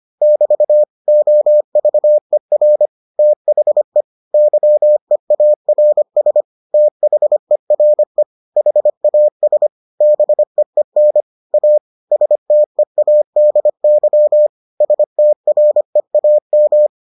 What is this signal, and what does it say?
Morse code, heard as a keyed tone, about 25 wpm, 600 Hz. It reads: <BT> OVER THE YEARS THERE HAS BEEN A STEADY STREAM